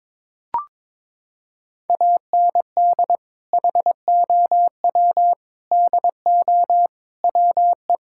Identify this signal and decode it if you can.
Morse code, heard as a keyed tone, about 21 wpm, 700 Hz. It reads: AND HOW DO WE
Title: Sentences from Top 1600 Words 22wpm